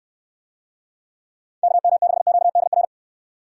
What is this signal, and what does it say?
Morse code, heard as a keyed tone, about 68 wpm, 700 Hz. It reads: KM6QDR